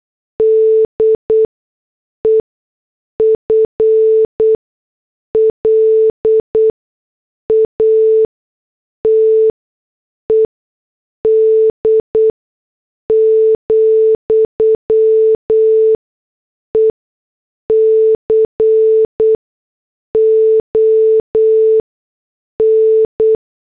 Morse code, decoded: DEFLATED, ECON